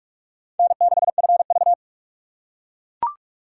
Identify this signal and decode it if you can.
Morse code, heard as a keyed tone, about 43 wpm, 700 Hz. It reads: N6FV